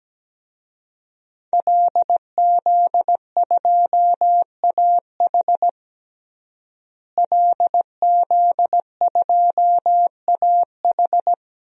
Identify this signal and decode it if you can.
Morse code, heard as a keyed tone, about 17 wpm, 700 Hz. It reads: LZ2AH LZ2AH
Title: DXCC-100 Call Signs - 5 Characters - Repeated 2x - 17wpm